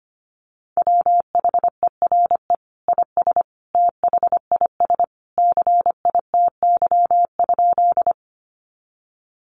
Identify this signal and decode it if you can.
Morse code, heard as a keyed tone, about 25 wpm, 700 Hz. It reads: WHERE IS THIS CITY?
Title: Sentences from Top 300 Words 25wpm